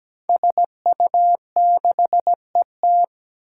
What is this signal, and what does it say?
Morse code, heard as a keyed tone, about 17 wpm, 700 Hz. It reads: SU6ET
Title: Five Challenging Letter-Number Combinations 17wpm